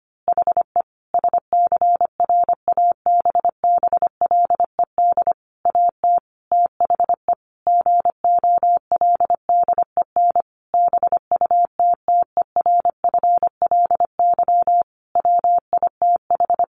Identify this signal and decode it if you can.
Morse code, heard as a keyed tone, about 25 wpm, 700 Hz. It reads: HE SCRABBLED AT THE GOLDEN BUTTERFLY WITH
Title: A Sound Of Thunder 25wpm